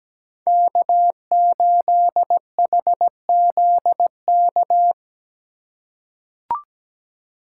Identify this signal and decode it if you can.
Morse code, heard as a keyed tone, about 17 wpm, 700 Hz. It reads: K8HZK
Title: USA Call Signs - 3x Character Spacing - 17wpm